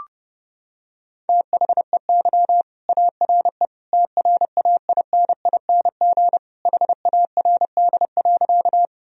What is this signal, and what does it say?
Morse code, heard as a keyed tone, about 32 wpm, 700 Hz. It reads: THEY ARE TRAINING HARD.
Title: Sentences from Top 1000 Words - Review Part 4 - 30wpm